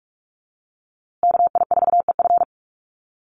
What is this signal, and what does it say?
Morse code, heard as a keyed tone, about 44 wpm, 700 Hz. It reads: KI4EF
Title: USA Call Signs Repeated 2x 45wpm